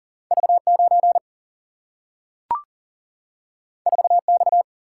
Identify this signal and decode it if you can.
Morse code, heard as a keyed tone, about 40 wpm, 700 Hz. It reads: V 9 4 X
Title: Single-Letter Single-Number 40wpm